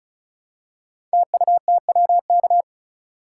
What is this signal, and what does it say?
Morse code, heard as a keyed tone, about 36 wpm, 700 Hz. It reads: TUTWK
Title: Code Groups - 5 Letters - 35wpm